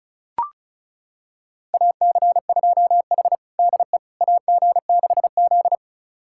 Morse code, heard as a keyed tone, about 34 wpm, 700 Hz. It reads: AC2H DE AG6Z
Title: USA Callsign DE Callsign - 4 Characters - 35wpm